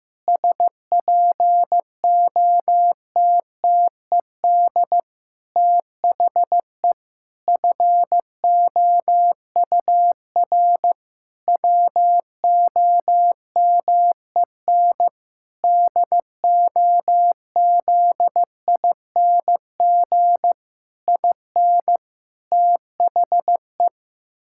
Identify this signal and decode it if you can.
Morse code, heard as a keyed tone, about 15 wpm, 700 Hz. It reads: SPOTTED THE FOUR WOMEN DOZING IN THE
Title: Pangram Challenge v2 - 15wpm